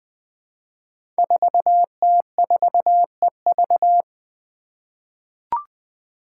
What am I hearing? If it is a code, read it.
Morse code, heard as a keyed tone, about 20 wpm, 700 Hz. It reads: 4T4EV
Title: Five Challenging Letter-Number Combinations 20wpm